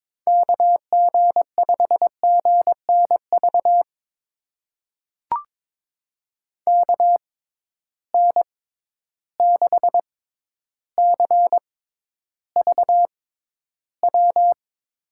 Morse code, heard as a keyed tone, about 22 wpm, spaced as for 11 wpm, 700 Hz. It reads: KG5GNV KN6CVW
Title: USA Call Signs - 6 Characters - 6x Character Spacing - 22wpm